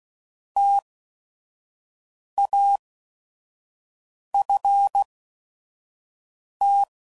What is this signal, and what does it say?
Morse code, heard as a keyed tone, about 16 wpm, spaced as for 5 wpm, 800 Hz. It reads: TAFT